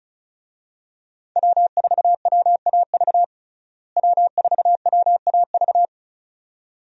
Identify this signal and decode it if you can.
Morse code, heard as a keyed tone, about 34 wpm, 700 Hz. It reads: W4WAV W4WAV
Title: USA Call Signs - 5 Characters - Repeated 2x - 35wpm